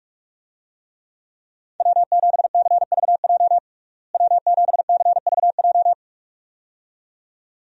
Morse code, heard as a keyed tone, about 42 wpm, 700 Hz. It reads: W7CVJ W7CVJ
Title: USA Call Signs - 5 Characters - Repeated 2x - 45wpm